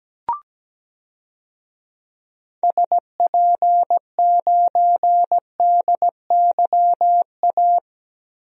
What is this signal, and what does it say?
Morse code, heard as a keyed tone, about 17 wpm, 700 Hz. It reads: SP9DYA
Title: DXCC-100 Call Signs - 6 Characters - 17wpm